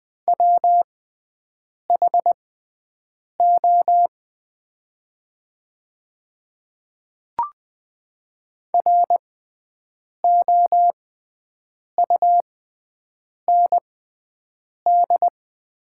Morse code, heard as a keyed tone, about 21 wpm, spaced as for 7 wpm, 700 Hz. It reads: WHO ROUND
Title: 6x Character Spacing - Top 200 Words - 20wpm